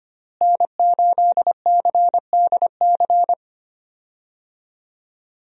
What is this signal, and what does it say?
Morse code, heard as a keyed tone, about 25 wpm, 700 Hz. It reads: N8CDC
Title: USA Call Signs - 5 Characters - Repeated 2x - 25wpm